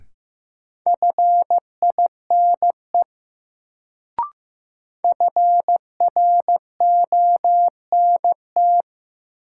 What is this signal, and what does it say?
Morse code, heard as a keyed tone, about 15 wpm, 700 Hz. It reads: FINE FRONT
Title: Sentences from Top 500 Words - New Word Drill - 15wpm